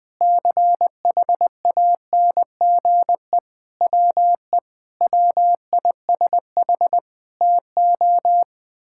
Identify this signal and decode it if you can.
Morse code, heard as a keyed tone, about 21 wpm, 700 Hz. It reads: CHANGE WE WISH TO
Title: Sets of 8 Words - 20wpm